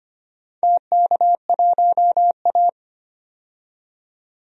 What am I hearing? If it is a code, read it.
Morse code, heard as a keyed tone, about 25 wpm, 700 Hz. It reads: TK1A
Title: DXCC-100 Call Signs - 4 Characters - 25wpm